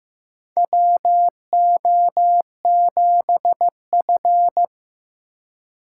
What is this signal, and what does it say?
Morse code, heard as a keyed tone, about 15 wpm, 700 Hz. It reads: WO7F
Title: USA Call Signs - 4 Characters - 3x Character Spacing - 15wpm